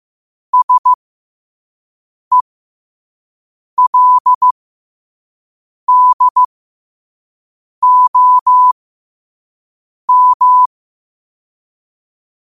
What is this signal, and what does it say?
Morse code, heard as a keyed tone, about 15 wpm, spaced as for 5 wpm, 1000 Hz. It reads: SELDOM
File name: qotdfiles / 5wpm